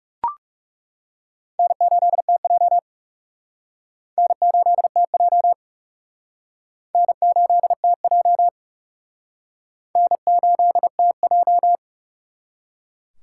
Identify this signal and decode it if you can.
Morse code, heard as a keyed tone, about 35 wpm, 700 Hz. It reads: N8TJ N8TJ N8TJ N8TJ
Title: Speed Racing - USA 1x2 Call Signs 30wpm